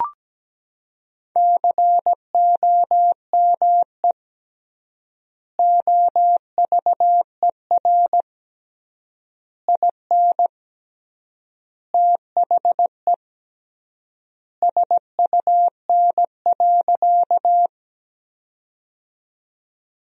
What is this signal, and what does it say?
Morse code, heard as a keyed tone, about 17 wpm, 700 Hz. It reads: COME OVER IN THE SUN.
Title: 3x Word-Spacing - Sentences from Top 300 Words - 17wpm